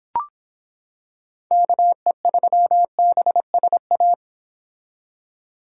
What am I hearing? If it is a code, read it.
Morse code, heard as a keyed tone, about 26 wpm, 700 Hz. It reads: KE3BSA